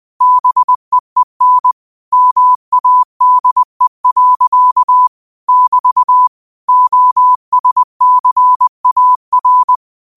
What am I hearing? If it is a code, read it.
Morse code, heard as a keyed tone, about 20 wpm, 1000 Hz. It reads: BEEN MADE. <BT> OSCAR